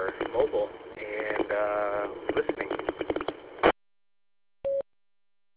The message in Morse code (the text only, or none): none